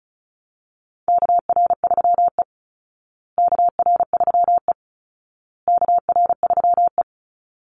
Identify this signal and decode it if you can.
Morse code, heard as a keyed tone, about 34 wpm, 700 Hz. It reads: KR3E KR3E KR3E
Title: Call Signs Repeated 2x - 35wpm